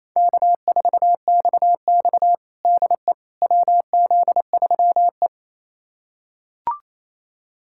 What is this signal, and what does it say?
Morse code, heard as a keyed tone, about 28 wpm, 700 Hz. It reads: K4XX DE WZ3E
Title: USA Callsign DE Callsign - 4 Characters - 28wpm